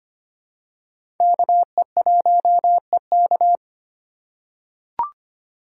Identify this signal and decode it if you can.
Morse code, heard as a keyed tone, about 25 wpm, 700 Hz. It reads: KE1EK
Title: USA Call Signs - 5 Characters - 3x Character Spacing - 25wpm